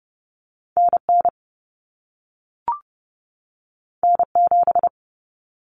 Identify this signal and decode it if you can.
Morse code, heard as a keyed tone, about 30 wpm, 700 Hz. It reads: NN N7
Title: Common Cut Numbers - 2-Digit 30wpm